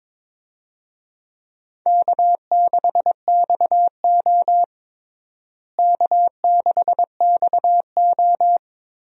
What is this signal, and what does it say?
Morse code, heard as a keyed tone, about 22 wpm, 700 Hz. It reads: K6XO K6XO
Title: USA Call Signs - 4 Characters - Repeated 3x - 22wpm